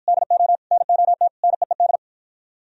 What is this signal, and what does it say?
Morse code, heard as a keyed tone, about 50 wpm, 700 Hz. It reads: DO NOT NEED